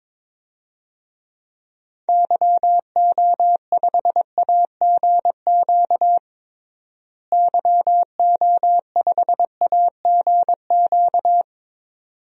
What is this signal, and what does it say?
Morse code, heard as a keyed tone, about 21 wpm, 700 Hz. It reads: YO5AGQ YO5AGQ
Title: DXCC-100 Call Signs - 6 Characters - Repeated 2x - 22wpm